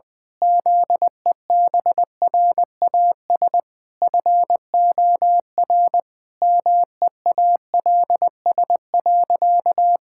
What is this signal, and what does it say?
Morse code, heard as a keyed tone, about 20 wpm, 700 Hz. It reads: ZEBRAS FOR MEALS.
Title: Pangram Challenge v2 - 20wpm